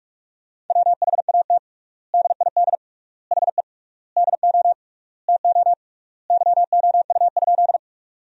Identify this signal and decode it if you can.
Morse code, heard as a keyed tone, about 46 wpm, 700 Hz. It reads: WHAT DID HE DO TO YOU?
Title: 3x Word-Spacing - Sentences from Top 100 Words - 45wpm